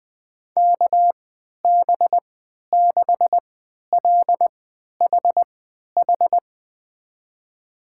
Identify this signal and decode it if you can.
Morse code, heard as a keyed tone, about 20 wpm, spaced as for 11 wpm, 700 Hz. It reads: KB6LHH
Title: USA Call Signs - 3x Character Spacing - 20wpm